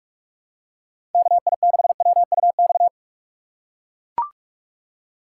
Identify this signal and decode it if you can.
Morse code, heard as a keyed tone, about 44 wpm, 700 Hz. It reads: KI6WUX
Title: USA Call Signs - 6x Character Spacing - 45wpm